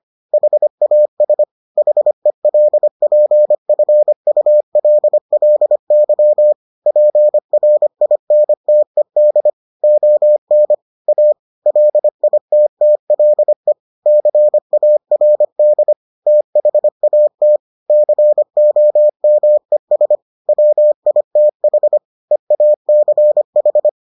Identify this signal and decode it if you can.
Morse code, heard as a keyed tone, about 25 wpm, 600 Hz. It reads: HAS HELPFULLY PRINTED ON A LITTLE CARD THAT COMES WITH EACH